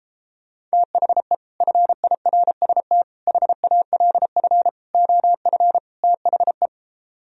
Morse code, heard as a keyed tone, about 33 wpm, 700 Hz. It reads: THE FIRST HALF OF THE